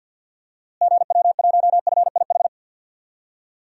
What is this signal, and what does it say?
Morse code, heard as a keyed tone, about 50 wpm, 700 Hz. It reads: GW1FIH
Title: DXCC-100 Call Signs - 6 Characters - Repeated 3x - 50wpm